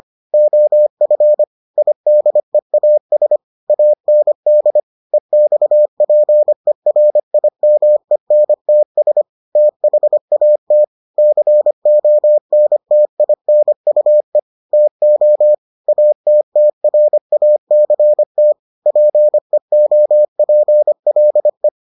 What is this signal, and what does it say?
Morse code, heard as a keyed tone, about 25 wpm, 600 Hz. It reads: OF IDEAS AND EXPERIMENTS THAT CONTINUE TO ATTRACT PEOPLE